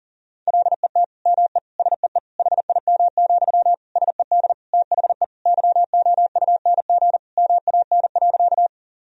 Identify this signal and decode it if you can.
Morse code, heard as a keyed tone, about 40 wpm, 700 Hz. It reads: LET ME SEE HIM, SED THE YOUNG MAN.
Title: Sentences from Top 400 Words and Abbreviations 40wpm